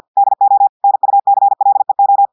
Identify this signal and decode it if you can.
Morse code, heard as a keyed tone, about 50 wpm, 800 Hz. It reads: DO NUCLEO
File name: audios / portugais / portugais texte en clair 1 vitesse 50 mots minutes